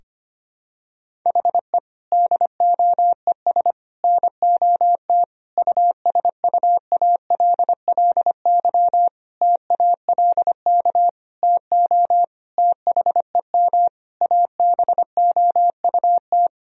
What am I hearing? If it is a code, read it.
Morse code, heard as a keyed tone, about 25 wpm, 700 Hz. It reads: HE DOES NOT USUALLY TALK TO THEM ABOUT
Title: Sentences from Top 1000 Words - Review Part 3 - 25wpm